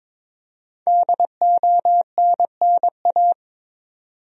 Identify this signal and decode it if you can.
Morse code, heard as a keyed tone, about 22 wpm, 700 Hz. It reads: DONNA
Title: USA Common Names - 22wpm